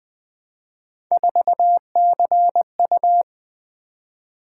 Morse code, heard as a keyed tone, about 20 wpm, 700 Hz. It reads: 4CU
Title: Three Letter-Number Combinations - 20wpm